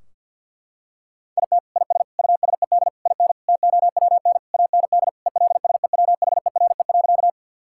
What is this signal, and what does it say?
Morse code, heard as a keyed tone, about 50 wpm, 700 Hz. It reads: IT IS USED IN TOWN AND ELSEWHERE.
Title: Sentences from Top 1600 Words 50wpm